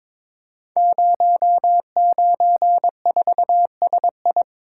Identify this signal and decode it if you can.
Morse code, heard as a keyed tone, about 22 wpm, 700 Hz. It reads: 094SI
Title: Code Groups - 5 Letter:Numbers - 22wpm